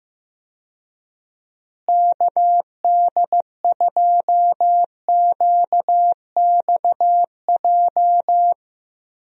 Morse code, heard as a keyed tone, about 15 wpm, 700 Hz. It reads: KD2QXJ
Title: USA Call Signs - 6 Characters - Repeated 2x - 15wpm